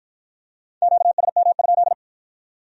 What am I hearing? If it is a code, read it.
Morse code, heard as a keyed tone, about 53 wpm, 700 Hz. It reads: QSM?